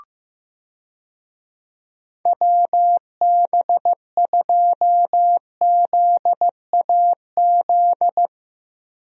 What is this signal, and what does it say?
Morse code, heard as a keyed tone, about 15 wpm, 700 Hz. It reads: WB2ZAZ
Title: USA Call Signs - 6 Characters - Repeated 3x - 15wpm